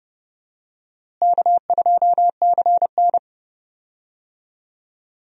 Morse code, heard as a keyed tone, about 30 wpm, 700 Hz. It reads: K2CN
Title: USA Call Signs - 4 Characters - Repeated 2x - 30wpm